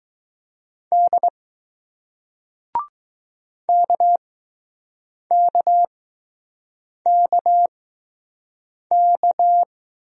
Morse code, heard as a keyed tone, about 20 wpm, 700 Hz. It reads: D K K K K